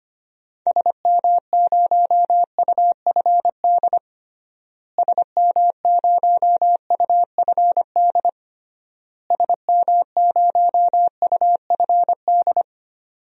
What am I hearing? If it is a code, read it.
Morse code, heard as a keyed tone, about 25 wpm, 700 Hz. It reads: SM0UFD SM0UFD SM0UFD
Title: DXCC-100 Call Signs - 6 Characters - Repeated 3x - 25wpm